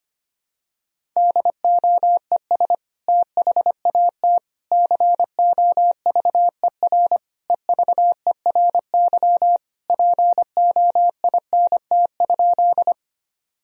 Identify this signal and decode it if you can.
Morse code, heard as a keyed tone, about 25 wpm, 700 Hz. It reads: DOES THAT COVER EVERY POINT?
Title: Sentences from Top 1000 Words - Review Part 1 - 25wpm